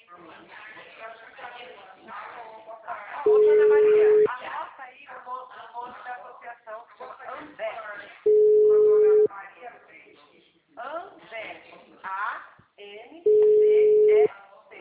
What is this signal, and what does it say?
Ringback tone (Europe, 425 Hz), with speech.
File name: out- -2008- - - .51242